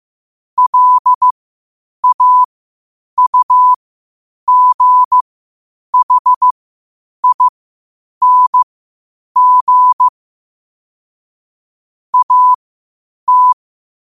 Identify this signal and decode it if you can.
Morse code, heard as a keyed tone, about 15 wpm, spaced as for 8 wpm, 1000 Hz. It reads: LAUGHING AT
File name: qotdfiles / 8wpm